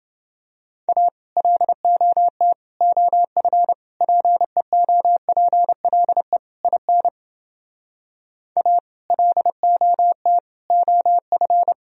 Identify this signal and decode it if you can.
Morse code, heard as a keyed tone, about 31 wpm, 700 Hz. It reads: A LOT OF PEOPLE IN A LOT OF